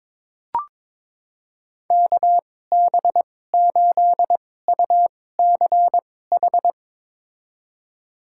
Morse code, heard as a keyed tone, about 22 wpm, spaced as for 16 wpm, 700 Hz. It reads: KB8UCH
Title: USA Call Signs - 6 Characters - 2x Character Spacing - 22wpm